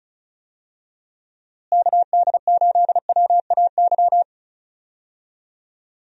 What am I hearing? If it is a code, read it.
Morse code, heard as a keyed tone, about 34 wpm, 700 Hz. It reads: KD8WAY
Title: USA Call Signs - 6 Characters - Repeated 2x - 35wpm